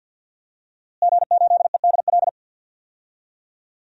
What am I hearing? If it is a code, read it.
Morse code, heard as a keyed tone, about 52 wpm, 700 Hz. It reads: G8EDL
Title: DXCC-100 Call Signs - 5 Characters - Repeated 2x - 50wpm